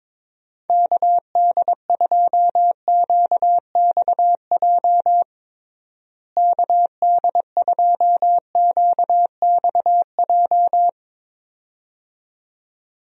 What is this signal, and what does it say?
Morse code, heard as a keyed tone, about 21 wpm, 700 Hz. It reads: KD2QXJ KD2QXJ
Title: USA Call Signs - 6 Characters - Repeated 2x - 22wpm